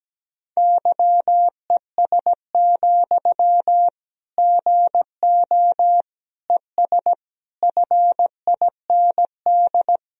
Morse code, heard as a keyed tone, about 17 wpm, 700 Hz. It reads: YES, GO ES FIND
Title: Sentences from Top 500 Words and Abbreviations 17wpm